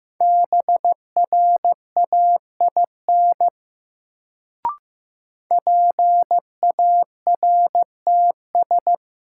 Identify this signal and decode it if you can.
Morse code, heard as a keyed tone, about 15 wpm, 700 Hz. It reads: BRAIN PARTS
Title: Sentences from Top 1000 Words - Review Part 6 - New Word Drill - 15wpm